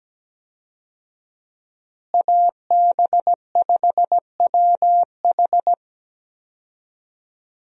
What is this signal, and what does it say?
Morse code, heard as a keyed tone, about 17 wpm, 700 Hz. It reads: AB5WH
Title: USA Call Signs - 5 Characters - Repeated 2x - 17wpm